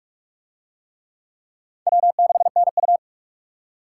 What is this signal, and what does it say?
Morse code, heard as a keyed tone, about 39 wpm, 700 Hz. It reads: W6NU